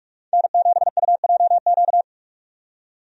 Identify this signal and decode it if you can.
Morse code, heard as a keyed tone, about 46 wpm, 700 Hz. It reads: N7UJQ
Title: USA Call Signs - 5 Characters - 45wpm